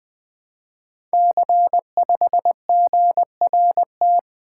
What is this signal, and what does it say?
Morse code, heard as a keyed tone, about 20 wpm, 700 Hz. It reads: C5GRT
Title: Code Groups - 5 Letter:Numbers - 20wpm